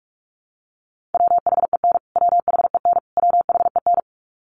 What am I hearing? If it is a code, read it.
Morse code, heard as a keyed tone, about 44 wpm, 700 Hz. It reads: WHEN WHEN WHEN
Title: Top 500 Words Repeated 45wpm